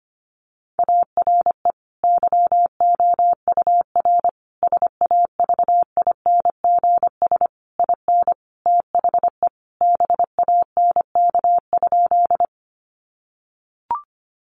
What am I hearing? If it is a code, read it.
Morse code, heard as a keyed tone, about 25 wpm, 700 Hz. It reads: ARE YOUR SAVINGS IN THE BANK?
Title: Sentences from Top 1200 Words 25wpm